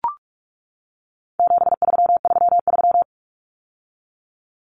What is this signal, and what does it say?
Morse code, heard as a keyed tone, about 44 wpm, 700 Hz. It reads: 7333
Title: Numbers - 4-Digit 45wpm